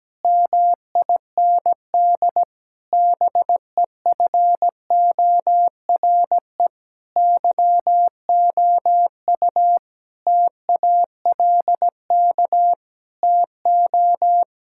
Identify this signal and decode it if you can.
Morse code, heard as a keyed tone, about 17 wpm, 700 Hz. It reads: MIND BEFORE YOU TALK TO
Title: Sentences from Top 1000 Words - Review Part 6 - 17wpm